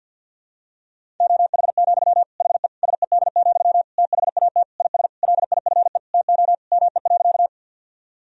Morse code, heard as a keyed tone, about 47 wpm, 700 Hz. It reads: OH, HE SED, THAT IS LIFE TO ME.
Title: Sentences from Top 500 Words and Abbreviations 50wpm